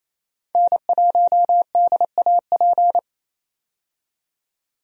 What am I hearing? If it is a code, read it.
Morse code, heard as a keyed tone, about 28 wpm, 700 Hz. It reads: N1DAP
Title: USA Call Signs - 5 Characters - 28wpm